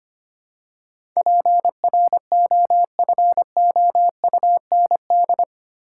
Morse code, heard as a keyed tone, about 25 wpm, 700 Hz. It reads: PROFOUND
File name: 8 Letter Words - First 3 Letters Given - 25wpm